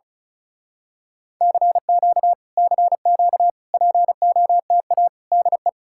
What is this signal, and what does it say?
Morse code, heard as a keyed tone, about 33 wpm, 700 Hz. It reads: CQ CQ POTA DE
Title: Speed-Racing Common QSO Phrases Encore - 35wpm